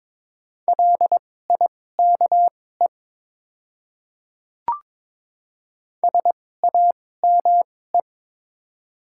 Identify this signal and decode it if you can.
Morse code, heard as a keyed tone, about 21 wpm, spaced as for 16 wpm, 700 Hz. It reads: LIKE SAME
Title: 2x Character Spacing - Top 200 Words - 22wpm